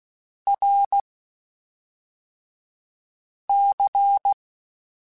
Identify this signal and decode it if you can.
Morse code, heard as a keyed tone, about 16 wpm, 800 Hz. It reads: R C